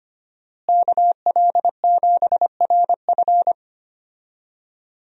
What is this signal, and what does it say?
Morse code, heard as a keyed tone, about 25 wpm, 700 Hz. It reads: KL7RF
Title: DXCC-100 Call Signs - 5 Characters - Repeated 2x - 25wpm